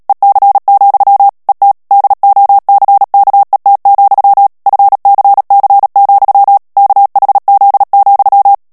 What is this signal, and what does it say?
Morse code, heard as a keyed tone, about 37 wpm, 800 Hz. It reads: EC, ET DOCKET, FCC, KHZ,
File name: morse mp3 / ANGLAIS texte clair 3 vitesse 35 mots minute